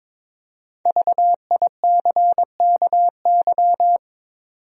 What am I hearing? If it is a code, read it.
Morse code, heard as a keyed tone, about 22 wpm, 700 Hz. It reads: VICKY
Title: USA Common Names - Max 5 Letters - 22wpm